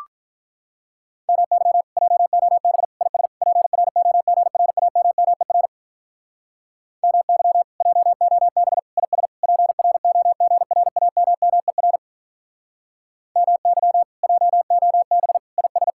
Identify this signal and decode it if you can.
Morse code, heard as a keyed tone, about 46 wpm, 700 Hz. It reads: MY JOB IS PROGRAMMER MY JOB IS PROGRAMMER MY JOB IS